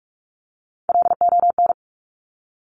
Morse code, heard as a keyed tone, about 46 wpm, 690 Hz. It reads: LON
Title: Top Three Letter Combinations in Words 45wpm